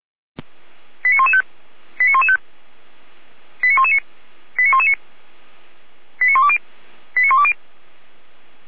Five-tone selective call (ZVEI), heard as quick runs of five tones, tones of about 70 ms. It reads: 89115 89115 89119 89119 89120 89120